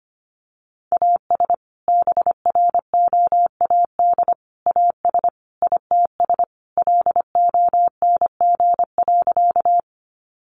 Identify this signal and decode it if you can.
Morse code, heard as a keyed tone, about 25 wpm, 700 Hz. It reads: AS BROAD AS ITS LONG.
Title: Sentences from Top 1000 Words 25wpm